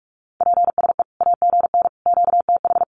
Morse code, heard as a keyed tone, about 44 wpm, 700 Hz. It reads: PSE AGN QTH
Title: Common QSO Phrases 45wpm